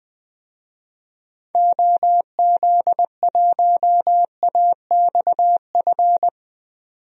Morse code, heard as a keyed tone, about 21 wpm, 700 Hz. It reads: OZ1AXF
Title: DXCC-100 Call Signs - 6 Characters - 20wpm